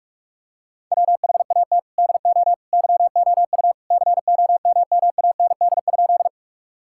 Morse code, heard as a keyed tone, about 46 wpm, 700 Hz. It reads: WHAT DO YOU COMMAND?
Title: Sentences from Top 1600 Words 45wpm